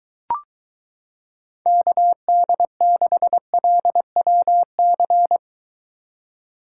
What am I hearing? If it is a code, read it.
Morse code, heard as a keyed tone, about 23 wpm, 700 Hz. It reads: KD6LWC